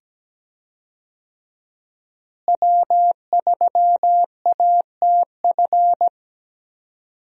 Morse code, heard as a keyed tone, about 17 wpm, 700 Hz. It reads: W3ATF
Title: USA Call Signs - 5 Characters - Repeated 2x - 17wpm